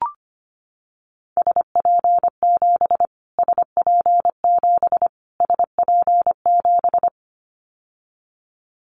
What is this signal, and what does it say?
Morse code, heard as a keyed tone, about 25 wpm, 700 Hz. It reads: SP7 SP7 SP7
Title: Callsign Prefixes for Top 25 DXCC - Repeated 3x - 25wpm